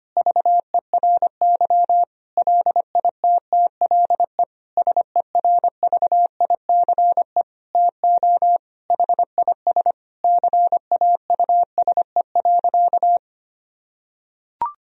Morse code, heard as a keyed tone, about 25 wpm, 700 Hz. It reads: VERY LITTLE SERVICE TO HIS CAUSE.
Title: Sentences from Top 1000 Words - Review Part 4 - 25wpm